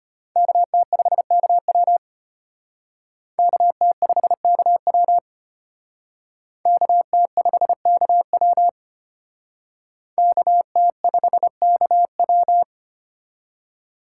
Morse code, heard as a keyed tone, about 31 wpm, 700 Hz. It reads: KT5KW KT5KW KT5KW KT5KW
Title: Speed Racing - USA Call Signs 25wpm